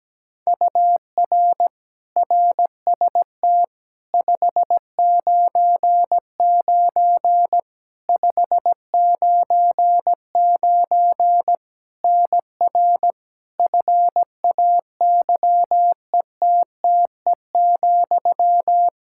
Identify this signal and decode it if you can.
Morse code, heard as a keyed tone, about 17 wpm, 700 Hz. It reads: UR RST 599 599 NR FAYETTE,
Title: Practice QSO - Exchange 1 - Beginning - 17wpm